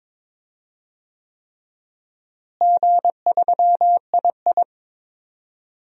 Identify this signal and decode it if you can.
Morse code, heard as a keyed tone, about 21 wpm, 700 Hz. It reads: G3II
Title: DXCC-100 Call Signs - 4 Characters - Repeated 3x - 22wpm